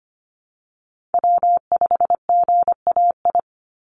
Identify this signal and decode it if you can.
Morse code, heard as a keyed tone, about 25 wpm, 700 Hz. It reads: W5GAI
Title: Call Signs Repeated 2x - 25wpm